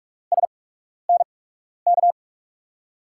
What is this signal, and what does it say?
Morse code, heard as a keyed tone, about 42 wpm, spaced as for 12 wpm, 700 Hz. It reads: SNK
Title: USA Call Signs - 5 Characters - 8x Character Spacing - 45wpm